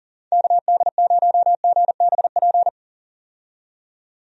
Morse code, heard as a keyed tone, about 40 wpm, 700 Hz. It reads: KD0GBP
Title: USA Call Signs - 6 Characters - 40wpm